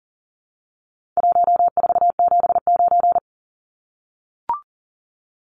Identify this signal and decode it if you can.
Morse code, heard as a keyed tone, about 40 wpm, 700 Hz. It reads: 1479
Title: Numbers - Contest Style - 4-Digit 40wpm